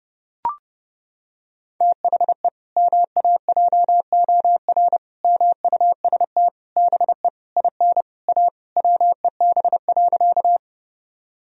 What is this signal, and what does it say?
Morse code, heard as a keyed tone, about 29 wpm, 700 Hz. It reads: THE MAJOR MUST BE IN A WEB.
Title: Sentences from Top 1000 Words - Review Part 4 - 30wpm